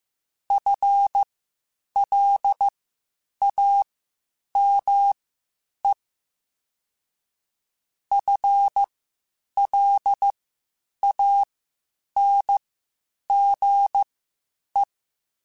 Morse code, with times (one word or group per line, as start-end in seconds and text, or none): 0.5-5.9 FLAME
8.1-14.8 FLANGE